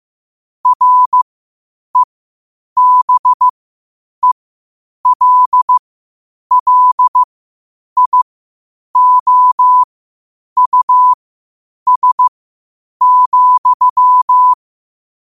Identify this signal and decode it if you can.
Morse code, heard as a keyed tone, about 15 wpm, spaced as for 8 wpm, 1000 Hz. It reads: REBELLIOUS,
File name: qotdfiles / 8wpm